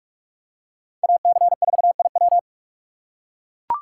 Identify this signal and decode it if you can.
Morse code, heard as a keyed tone, about 41 wpm, 700 Hz. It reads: AC4IW